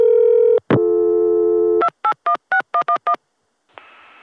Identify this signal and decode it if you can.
DTMF, heard as telephone touch tones, tones of about 80 ms. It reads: 2816111